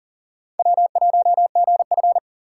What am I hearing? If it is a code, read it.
Morse code, heard as a keyed tone, about 40 wpm, 690 Hz. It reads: W1GF
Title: USA Call Signs - 4x Character Spacing - 40wpm